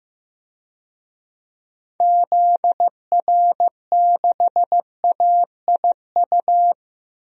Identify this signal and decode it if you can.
Morse code, heard as a keyed tone, about 15 wpm, 700 Hz. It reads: ZR6AIU
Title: DXCC-100 Call Signs - 6 Characters - 15wpm